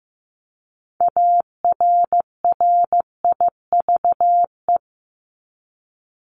Morse code, heard as a keyed tone, about 15 wpm, 700 Hz. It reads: ARRIVE
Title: ED Suffix - Top 100 Words with Base Words 15wpm